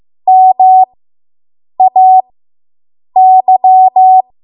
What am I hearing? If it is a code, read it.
Morse code, heard as a keyed tone, about 15 wpm, spaced as for 7 wpm, 750 Hz. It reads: MAY